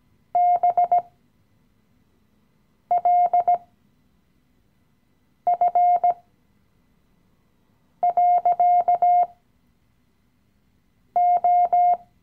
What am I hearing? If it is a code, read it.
Morse code, heard as a keyed tone, about 17 wpm, spaced as for 4 wpm, 710 Hz. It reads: BLF.O
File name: CW-Online-Kurs / Anfänger / Lektion 015